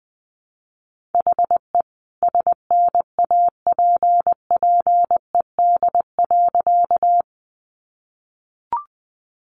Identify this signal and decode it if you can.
Morse code, heard as a keyed tone, about 20 wpm, 700 Hz. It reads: HE SNAPPED.
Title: A Sound Of Thunder 20wpm